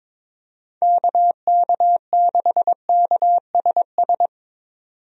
Morse code, heard as a keyed tone, about 22 wpm, 700 Hz. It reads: KK6KSS
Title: USA Call Signs - 6 Characters - 22wpm